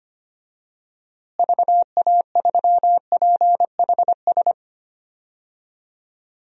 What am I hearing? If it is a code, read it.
Morse code, heard as a keyed tone, about 25 wpm, 700 Hz. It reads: VA3PHS
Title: DXCC-100 Call Signs - 6 Characters - Repeated 3x - 25wpm